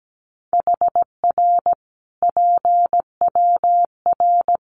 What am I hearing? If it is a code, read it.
Morse code, heard as a keyed tone, about 17 wpm, 700 Hz. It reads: HR PWR
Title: Common QSO Phrases 17wpm